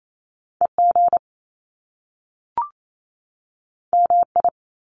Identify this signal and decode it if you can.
Morse code, heard as a keyed tone, about 28 wpm, 700 Hz. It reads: EG MI
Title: Top Two Letter Combinations in Words 28wpm